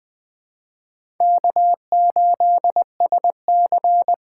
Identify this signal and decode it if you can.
Morse code, heard as a keyed tone, about 20 wpm, 700 Hz. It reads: K8SC
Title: USA Call Signs - 8x Character Spacing - 20wpm